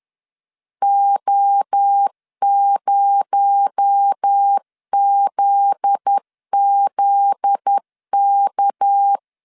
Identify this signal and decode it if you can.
Morse code, heard as a keyed tone, about 10 wpm, 800 Hz. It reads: O0ZZK
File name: CW / DET / 1 9 1500C S A 10wpm 4511